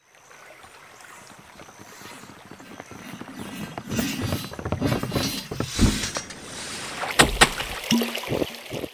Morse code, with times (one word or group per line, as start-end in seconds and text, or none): none